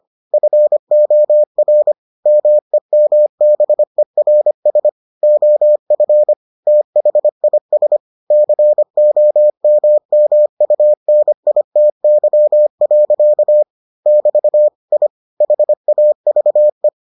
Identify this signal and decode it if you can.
Morse code, heard as a keyed tone, about 25 wpm, 600 Hz. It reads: FOR MEMBERS OF THIS COMMUNITY. <BT> I HAVE